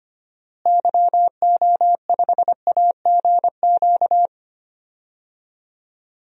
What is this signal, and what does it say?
Morse code, heard as a keyed tone, about 25 wpm, 700 Hz. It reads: YO5AGQ
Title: DXCC-100 Call Signs - 6 Characters - Repeated 2x - 25wpm